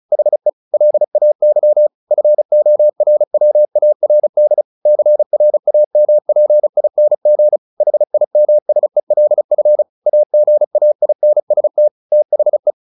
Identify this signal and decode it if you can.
Morse code, heard as a keyed tone, about 34 wpm, 600 Hz. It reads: HE LAY FORWARD CRAMPING HIMSELF AGAINST THE
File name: the old man and the sea 2 35wpm